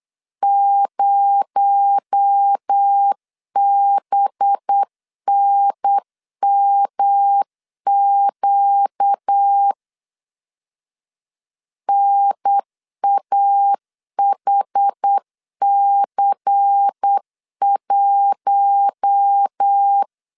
Morse code, heard as a keyed tone, about 8 wpm, 800 Hz. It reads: 0BNMQ NAHC1
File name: LEZIONI / 6 PKX0 / 6 PKX0 8wpm 3341